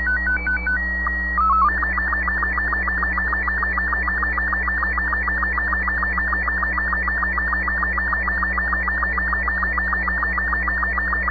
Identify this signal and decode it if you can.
SSTV (slow-scan television), Robot 36.
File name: sat / AO-51 14Jul2010 SSTV